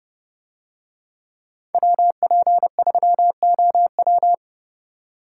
Morse code, heard as a keyed tone, about 30 wpm, 700 Hz. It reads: WP3OW